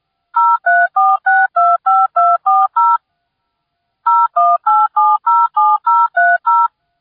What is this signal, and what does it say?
DTMF, heard as telephone touch tones, tones of about 220 ms. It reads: 034625240 018707030